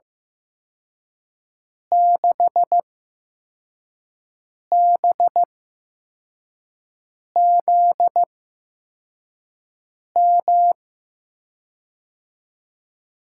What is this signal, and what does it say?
Morse code, heard as a keyed tone, about 15 wpm, spaced as for 4 wpm, 700 Hz. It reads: 6BZM
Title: USA Call Signs - 6 Characters - 8x Character Spacing - 15wpm